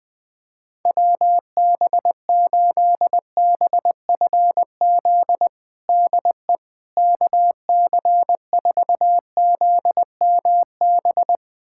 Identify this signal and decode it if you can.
Morse code, heard as a keyed tone, about 20 wpm, 700 Hz. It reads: WB8BFZ DE KC4ZMB
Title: USA Callsign DE Callsign - 6 Characters - 20wpm